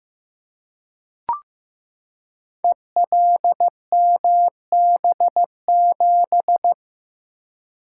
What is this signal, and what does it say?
Morse code, heard as a keyed tone, about 15 wpm, 700 Hz. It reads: ELMB7
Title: Code Groups - 5 Letter:Numbers - 15wpm